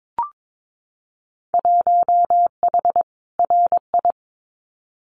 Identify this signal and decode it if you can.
Morse code, heard as a keyed tone, about 22 wpm, 700 Hz. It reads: 1H RI
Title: Winter Field Day Exchanges 22wpm